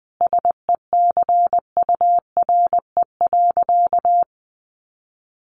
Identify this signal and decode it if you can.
Morse code, heard as a keyed tone, about 20 wpm, 700 Hz. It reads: SECURE.
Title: Sentences from Top 1200 Words 20wpm